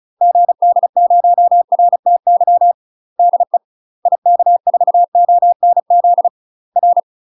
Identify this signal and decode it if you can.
Morse code, heard as a keyed tone, about 34 wpm, 700 Hz. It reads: GD0RTY DE IK4ONZ R
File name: UK MASTER QSOs TRAINING / 35WPM / UK MASTER QSOs 0025